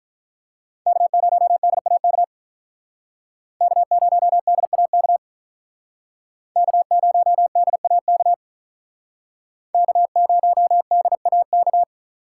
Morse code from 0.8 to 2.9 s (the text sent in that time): K0DAK